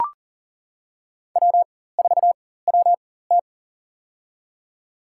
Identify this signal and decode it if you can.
Morse code, heard as a keyed tone, about 40 wpm, spaced as for 19 wpm, 700 Hz. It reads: W4WT
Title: USA Call Signs - 4x Character Spacing - 40wpm